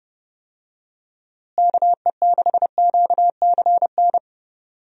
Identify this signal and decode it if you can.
Morse code, heard as a keyed tone, about 30 wpm, 700 Hz. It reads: KE6QCN
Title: USA Call Signs - 6 Characters - Repeated 2x - 30wpm